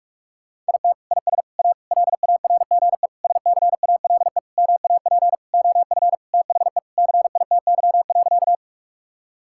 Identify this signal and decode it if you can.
Morse code, heard as a keyed tone, about 46 wpm, 700 Hz. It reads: IT IS A LARGE SCALE MAP OF THE CITY.
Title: Sentences from Top 1000 Words - Review Part 5 - 45wpm